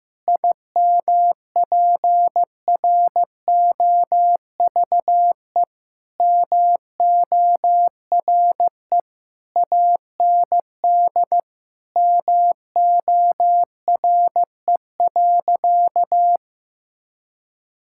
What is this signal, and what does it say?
Morse code, heard as a keyed tone, about 15 wpm, 700 Hz. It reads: IMPROVE MORE AND MORE.
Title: Sentences from Top 1000 Words - Review Part 5 - 15wpm